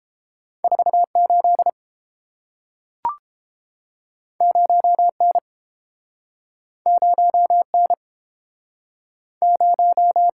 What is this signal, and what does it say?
Morse code, heard as a keyed tone, about 30 wpm, 700 Hz. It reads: 48 0N 0N 0